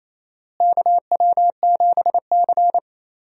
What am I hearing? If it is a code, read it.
Morse code, heard as a keyed tone, about 28 wpm, 700 Hz. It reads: KW7C
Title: USA Call Signs - 4 Characters - Repeated 2x - 28wpm